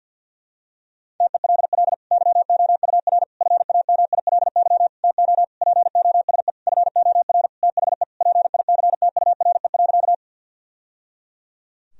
Morse code, heard as a keyed tone, about 53 wpm, 700 Hz. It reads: TELL YOUR FAMILY TO POSE FOR THE PICTURE.
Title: Sentences from Top 1000 Words - Review Part 2 - 50wpm